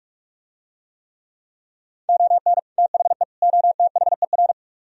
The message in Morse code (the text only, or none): ON THE OTHER